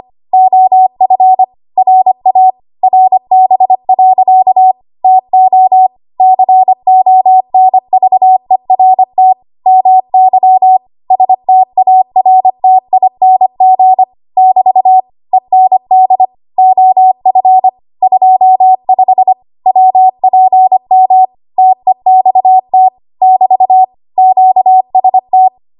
Morse code, heard as a keyed tone, about 25 wpm, 750 Hz. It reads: OF RA RB. TO CONVERT MY STARTING <BT> END OF 25 WPM TEXT <BT> QST
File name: Archive / 25 WPM / 25WPM